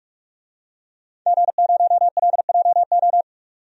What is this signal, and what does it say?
Morse code, heard as a keyed tone, about 44 wpm, 700 Hz. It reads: G0LJO
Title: DXCC-100 Call Signs - 5 Characters - 45wpm